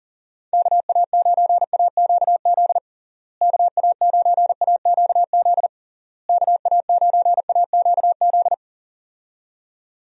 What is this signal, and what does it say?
Morse code, heard as a keyed tone, about 37 wpm, 700 Hz. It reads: KA9AQZ KA9AQZ KA9AQZ
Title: USA Call Signs - 6 Characters - Repeated 3x - 40wpm